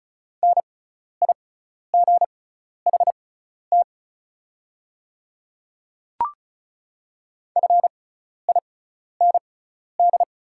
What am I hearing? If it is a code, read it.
Morse code, heard as a keyed tone, about 36 wpm, spaced as for 12 wpm, 700 Hz. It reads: NIGHT FIND
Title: 6x Character Spacing - Top 500 Words - 35wpm